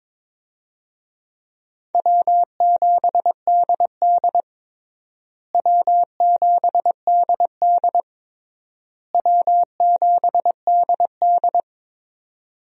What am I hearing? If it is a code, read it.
Morse code, heard as a keyed tone, about 21 wpm, 700 Hz. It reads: W7DD W7DD W7DD
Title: USA Call Signs - 4 Characters - Repeated 3x - 22wpm